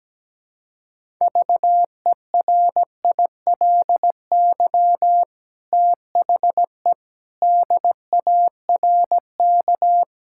Morse code, heard as a keyed tone, about 17 wpm, 700 Hz. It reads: VERILY THE DARK
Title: Pangram Challenge v2 - 17wpm